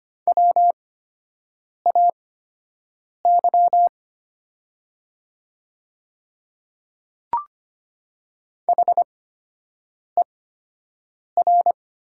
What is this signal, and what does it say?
Morse code, heard as a keyed tone, about 25 wpm, spaced as for 7 wpm, 700 Hz. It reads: WAY HER
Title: 8x Character Spacing - Top 500 Words - 25wpm